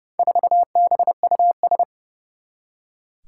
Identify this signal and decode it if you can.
Morse code, heard as a keyed tone, about 28 wpm, 700 Hz. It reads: 4BUS
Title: Four Challenging Letter-Number Combinations 30wpm